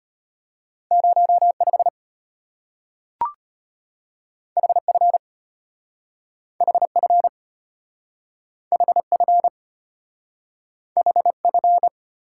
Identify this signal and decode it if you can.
Morse code, heard as a keyed tone, about 34 wpm, 700 Hz. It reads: 05 HF HF HF HF